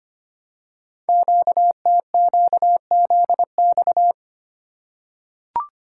Morse code, heard as a keyed tone, about 25 wpm, 700 Hz. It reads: QTQZX
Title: Code Groups - 5 Letters - 25wpm